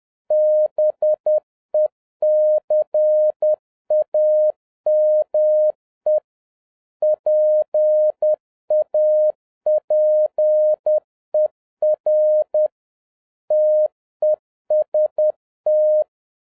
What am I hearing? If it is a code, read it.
Morse code, heard as a keyed tone, about 10 wpm, 610 Hz. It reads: BECAME PAPER TEST